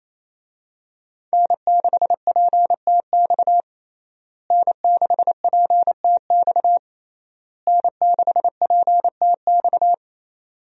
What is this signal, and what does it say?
Morse code, heard as a keyed tone, about 28 wpm, 700 Hz. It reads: N6PTX N6PTX N6PTX
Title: USA Call Signs - 5 Characters - Repeated 3x - 28wpm